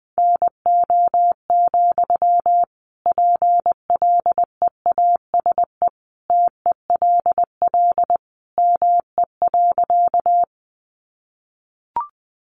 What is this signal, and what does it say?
Morse code, heard as a keyed tone, about 20 wpm, 700 Hz. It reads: NO, PLEASE TELL ME.
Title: Sentences from Top 1500 Words 20wpm